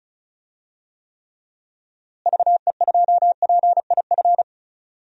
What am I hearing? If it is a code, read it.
Morse code, heard as a keyed tone, about 36 wpm, 700 Hz. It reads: VE2PIF
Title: DXCC-100 Call Signs - 6 Characters - Repeated 2x - 35wpm